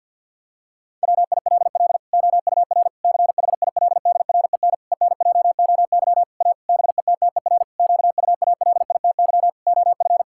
Morse code, heard as a keyed tone, about 46 wpm, 700 Hz. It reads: WILL OUR CHILDREN ENJOY A BETTER QUALITY OF
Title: Sentences from Top 1000 Words - Review Part 5 - 50wpm